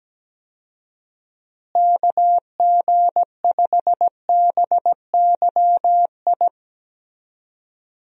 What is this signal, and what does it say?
Morse code, heard as a keyed tone, about 17 wpm, 700 Hz. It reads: KG5BYI
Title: USA Call Signs - 6 Characters - Repeated 3x - 17wpm